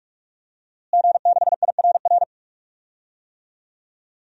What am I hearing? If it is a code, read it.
Morse code, heard as a keyed tone, about 45 wpm, 700 Hz. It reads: G6IRR
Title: DXCC-100 Call Signs - 5 Characters - Repeated 3x - 45wpm